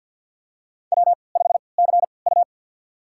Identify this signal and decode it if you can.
Morse code, heard as a keyed tone, about 51 wpm, spaced as for 28 wpm, 700 Hz. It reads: W5CU
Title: USA Call Signs - 4 Characters - 3x Character Spacing - 50wpm